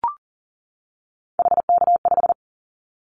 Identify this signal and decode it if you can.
Morse code, heard as a keyed tone, about 38 wpm, 700 Hz. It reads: HK5
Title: Callsign Prefixes for Top 100 DXCC - 40wpm